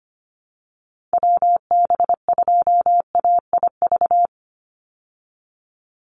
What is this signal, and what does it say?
Morse code, heard as a keyed tone, about 25 wpm, 700 Hz. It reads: WB2AIV
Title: Call Signs Repeated 2x - 25wpm